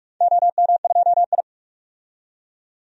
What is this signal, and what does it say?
Morse code, heard as a keyed tone, about 44 wpm, 700 Hz. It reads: OM2I
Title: DXCC-100 Call Signs - 4 Characters - 45wpm